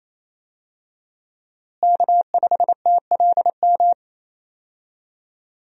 Morse code, heard as a keyed tone, about 27 wpm, 700 Hz. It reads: K5TLM
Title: USA Call Signs - 5 Characters - Repeated 3x - 28wpm